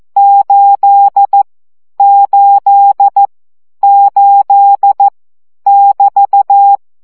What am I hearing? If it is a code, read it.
Morse code, heard as a keyed tone, about 14 wpm, 800 Hz. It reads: 8 8 8 <BT>
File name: CW gyakorlas 8 12 baud